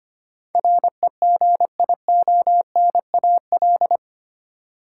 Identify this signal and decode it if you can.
Morse code, heard as a keyed tone, about 25 wpm, 700 Hz. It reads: REGIONAL
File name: Eight Letter Words - 25wpm